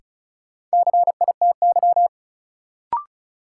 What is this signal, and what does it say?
Morse code, heard as a keyed tone, about 36 wpm, 700 Hz. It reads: CITY
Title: Sets of 2 Words - New Word Drill - 35wpm